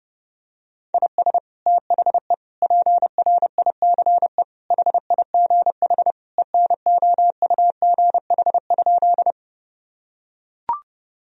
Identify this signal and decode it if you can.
Morse code, heard as a keyed tone, about 30 wpm, 700 Hz. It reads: IS THE PRICE HIGH ENOUGH?
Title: Sentences from Top 1000 Words - Review Part 4 - 30wpm